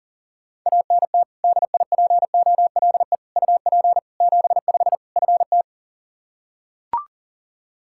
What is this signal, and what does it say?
Morse code, heard as a keyed tone, about 37 wpm, 700 Hz. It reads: ANT DIPOLE UP 75 FT
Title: K1USN SST QSO Practice 40wpm